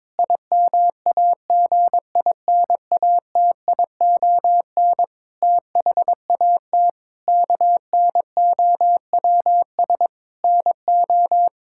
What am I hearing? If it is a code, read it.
Morse code, heard as a keyed tone, about 22 wpm, 700 Hz. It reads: IMAGINATION THAT KNOWS NO
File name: Difficult Sentences with Hints - 22wpm